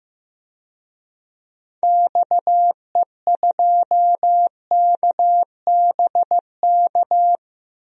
Morse code, heard as a keyed tone, about 15 wpm, 700 Hz. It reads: XE2KBK
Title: DXCC-100 Call Signs - 6 Characters - 15wpm